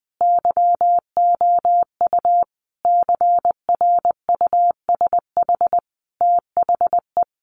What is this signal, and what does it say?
Morse code, heard as a keyed tone, about 20 wpm, 700 Hz. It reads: YOU CRUSH THE
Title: A Sound Of Thunder 20wpm